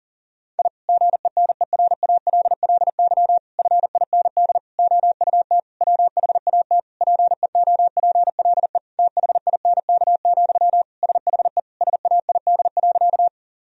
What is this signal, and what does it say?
Morse code, heard as a keyed tone, about 40 wpm, 690 Hz. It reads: I GENERALLY FIND OUT WHAT PEOPLE THINK, SHE SAID.
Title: Sentences from Top 1000 Words - Review Part 4 - 40wpm